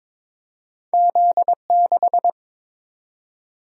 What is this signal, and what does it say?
Morse code, heard as a keyed tone, about 22 wpm, 700 Hz. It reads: Z6
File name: practice / 22wpm / Two Letter-Number Combinations - 22wpm